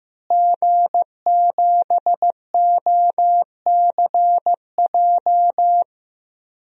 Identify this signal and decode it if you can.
Morse code, heard as a keyed tone, about 15 wpm, 700 Hz. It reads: G7OCJ
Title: DXCC-100 Call Signs - 5 Characters - Repeated 3x - 15wpm